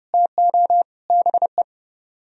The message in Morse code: TO BE

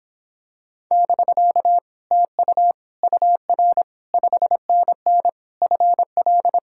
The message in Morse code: <BK> TU UR 5NN FL

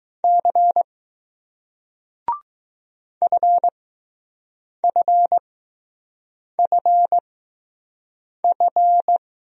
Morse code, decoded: C FFFF

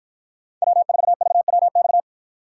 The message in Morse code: W4VWX

